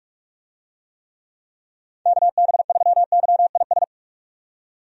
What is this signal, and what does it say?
Morse code, heard as a keyed tone, about 42 wpm, 700 Hz. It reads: KB3YIS
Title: USA Call Signs - 6 Characters - 45wpm